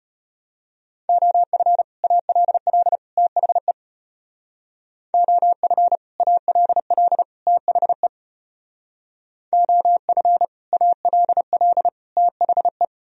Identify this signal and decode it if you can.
Morse code, heard as a keyed tone, about 31 wpm, 700 Hz. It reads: OF ALL THE OF ALL THE OF ALL THE
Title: Speed-Racing - Sets of 3 Words 25wpm